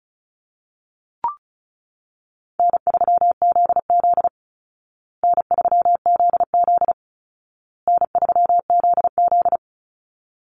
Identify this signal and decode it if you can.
Morse code, heard as a keyed tone, about 34 wpm, 700 Hz. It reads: N3ZZ N3ZZ N3ZZ
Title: Call Signs Repeated 2x - 35wpm